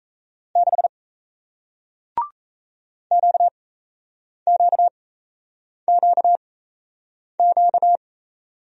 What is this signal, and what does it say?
Morse code, heard as a keyed tone, about 38 wpm, spaced as for 8 wpm, 700 Hz. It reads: 6 QQQQ